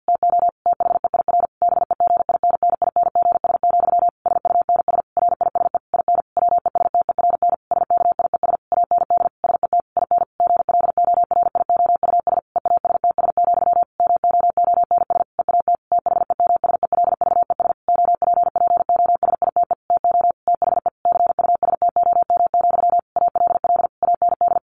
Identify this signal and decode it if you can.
Morse code, heard as a keyed tone, about 50 wpm, 700 Hz. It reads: TO THEIR BEGINNINGS, SUNS RISE IN WESTERN SKIES AND SET IN GLORIOUS EASTS, MOONS EAT THEMSELVES OPPOSITE TO THE CUSTOM, ALL AND
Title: A Sound Of Thunder 50wpm